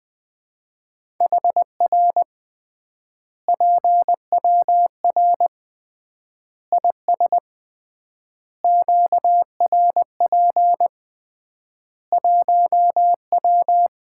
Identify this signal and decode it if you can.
Morse code, heard as a keyed tone, about 20 wpm, 700 Hz. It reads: HR PWR IS QRP 1W
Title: Common QSO Phrases - 3x Word-Spacing - 20wpm